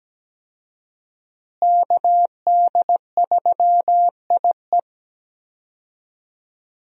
Morse code, heard as a keyed tone, about 17 wpm, 700 Hz. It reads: KD3IE
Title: USA Call Signs - 5 Characters - Repeated 3x - 17wpm